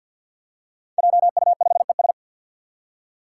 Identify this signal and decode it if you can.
Morse code, heard as a keyed tone, about 51 wpm, 700 Hz. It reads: JU5ES
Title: Five Challenging Letter-Number Combinations 50wpm